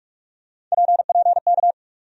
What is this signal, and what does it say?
Morse code, heard as a keyed tone, about 46 wpm, 700 Hz. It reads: PPK
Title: USA Call Sign Postfixes - 3 Letters 45wpm